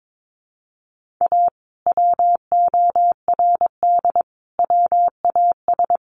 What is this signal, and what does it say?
Morse code, heard as a keyed tone, about 22 wpm, 700 Hz. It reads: A WORD WAS
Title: Sentences from Top 1400 Words 22wpm